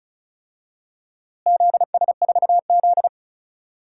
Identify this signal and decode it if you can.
Morse code, heard as a keyed tone, about 36 wpm, 690 Hz. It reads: ZS4Z
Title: DXCC-100 Call Signs - 4 Characters - Repeated 3x - 35wpm